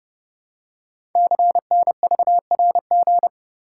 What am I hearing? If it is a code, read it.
Morse code, heard as a keyed tone, about 30 wpm, 700 Hz. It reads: CNVRG
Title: Code Groups - 5 Letters - 30wpm